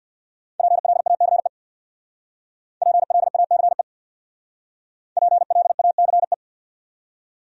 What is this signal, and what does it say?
Morse code, heard as a keyed tone, about 59 wpm, 700 Hz. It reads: PLACE PLACE PLACE
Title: Speed-Racing Top 300 Words 50wpm